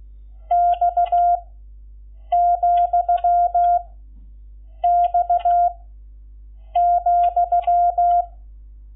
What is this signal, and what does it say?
Morse code, heard as a keyed tone, about 16 wpm, 700 Hz. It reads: X , X ,
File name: Lecciones / CLASE Nº11